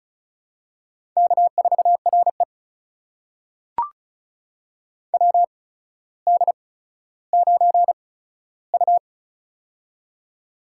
Morse code, heard as a keyed tone, about 36 wpm, spaced as for 15 wpm, 700 Hz. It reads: K4RE WD9U
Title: USA Call Signs - 4 Characters - 8x Character Spacing - 35wpm